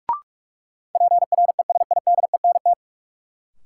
Morse code, heard as a keyed tone, about 44 wpm, 700 Hz. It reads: PRESIDENT
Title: Sentences from Top 1000 Words - New Word Drill - 45wpm